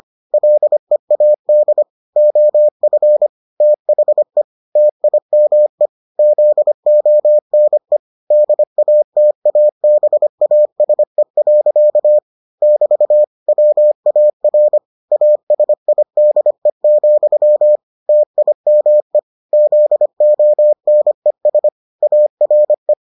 Morse code, heard as a keyed tone, about 25 wpm, 600 Hz. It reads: LEAD OF THE TIME ZONE DATABASE. <BT> WAR ASIDE, TIME ZONES ARE